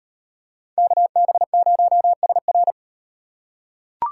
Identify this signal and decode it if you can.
Morse code, heard as a keyed tone, about 38 wpm, 700 Hz. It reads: KB0SR